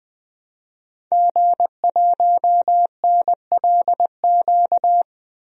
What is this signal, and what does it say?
Morse code, heard as a keyed tone, about 20 wpm, 700 Hz. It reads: G1NLQ
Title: DXCC-100 Call Signs - 5 Characters - 20wpm